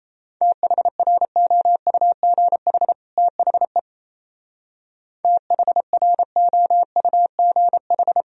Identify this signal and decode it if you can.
Morse code, heard as a keyed tone, about 35 wpm, 700 Hz. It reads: THROUGH THE THROUGH
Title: Speed-Racing - Sets of 2 Words 28wpm